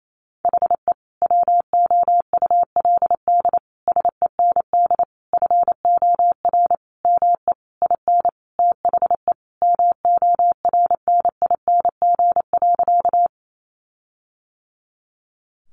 Morse code, heard as a keyed tone, about 28 wpm, 700 Hz. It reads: HE WOULD SEND FOR ME IN THE MORNING.
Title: Sentences from Top 1500 Words 28wpm